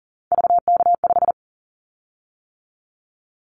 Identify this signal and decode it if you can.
Morse code, heard as a keyed tone, about 37 wpm, 690 Hz. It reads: VK5
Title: Callsign Prefixes for Top 100 DXCC - 40wpm